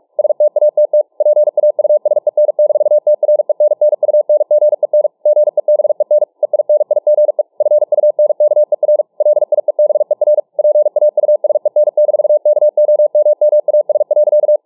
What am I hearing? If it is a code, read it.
Morse code, heard as a keyed tone, about 48 wpm, 600 Hz. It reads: STATT PAUSEN-TRENNUNGEN GEBEN EINIGE FUNKER LIEBER PAUSEN-KOMMAS.